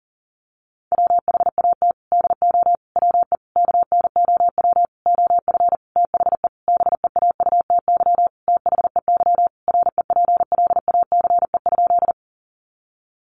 Morse code, heard as a keyed tone, about 40 wpm, 700 Hz. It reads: WHAT DO WE KNOW OF THE BEAUTY THEY REPLACE?
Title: Sentences from Top 1400 Words 40wpm